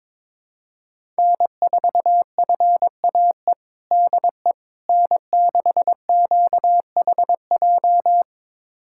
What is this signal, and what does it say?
Morse code, heard as a keyed tone, about 22 wpm, 700 Hz. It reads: N4FAE DE N6QHJ
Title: USA Callsign DE Callsign - 5 Characters - 22wpm